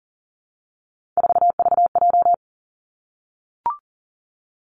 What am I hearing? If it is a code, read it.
Morse code, heard as a keyed tone, about 40 wpm, 700 Hz. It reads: 4VJ
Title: Three Challenging Letter-Number Combinations 40wpm